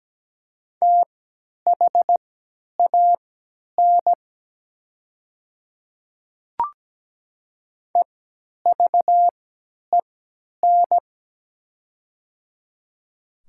Morse code, heard as a keyed tone, about 17 wpm, spaced as for 10 wpm, 700 Hz. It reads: THAN EVEN
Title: 3x Character Spacing - Top 200 Words - 17wpm